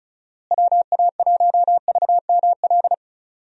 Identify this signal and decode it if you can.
Morse code, heard as a keyed tone, about 34 wpm, 700 Hz. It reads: WA1VML
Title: USA Call Signs - 6 Characters - Repeated 3x - 35wpm